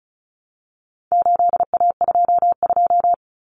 Morse code, heard as a keyed tone, about 34 wpm, 700 Hz. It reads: 8A22
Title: Common Cut Numbers - 4-Digit 35wpm